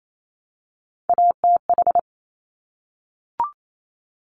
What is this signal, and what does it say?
Morse code, heard as a keyed tone, about 28 wpm, 700 Hz. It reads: ATH
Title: Top Three Letter Combinations in Words 28wpm